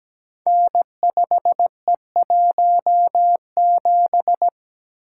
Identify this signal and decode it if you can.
Morse code, heard as a keyed tone, about 17 wpm, 700 Hz. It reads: N5E17
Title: Code Groups - 5 Letter:Numbers - 17wpm